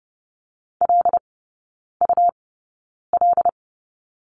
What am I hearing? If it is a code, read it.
Morse code, heard as a keyed tone, about 30 wpm, spaced as for 9 wpm, 700 Hz. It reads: LUL
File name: Lesson 19e - Letter w - Three Characters 30wpm